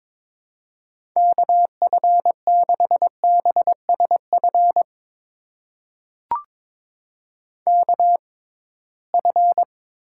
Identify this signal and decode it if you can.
Morse code, heard as a keyed tone, about 22 wpm, spaced as for 17 wpm, 700 Hz. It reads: KF6BSF KF
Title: USA Call Signs - 6 Characters - 6x Character Spacing - 22wpm